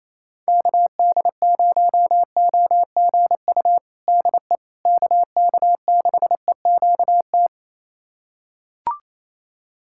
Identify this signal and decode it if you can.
Morse code, heard as a keyed tone, about 28 wpm, 700 Hz. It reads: KD0OGU DE KK6EQT
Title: USA Callsign DE Callsign - 6 Characters - 28wpm